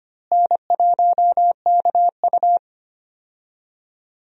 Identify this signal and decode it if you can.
Morse code, heard as a keyed tone, about 25 wpm, 700 Hz. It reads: N1KU
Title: USA Call Signs - 4 Characters - Repeated 2x - 25wpm